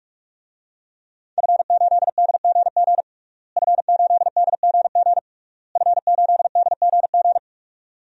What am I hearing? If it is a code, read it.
Morse code, heard as a keyed tone, about 46 wpm, 700 Hz. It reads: F8DGG F8DGG F8DGG
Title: DXCC-100 Call Signs - 5 Characters - Repeated 3x - 45wpm